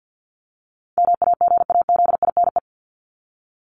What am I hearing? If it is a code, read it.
Morse code, heard as a keyed tone, about 50 wpm, 700 Hz. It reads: MAGAZINE
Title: Top 751-1000 Words 50wpm